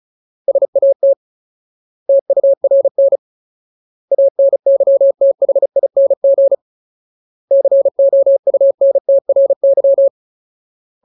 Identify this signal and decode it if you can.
Morse code, heard as a keyed tone, about 34 wpm, 550 Hz. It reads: SAT TURN ANYTHING COUNTRY
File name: Ultimate Word List / 35WPM Ultimate